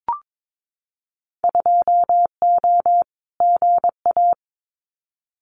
Morse code, heard as a keyed tone, about 22 wpm, 700 Hz. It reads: 2O GA
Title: Winter Field Day Exchanges Encore 22wpm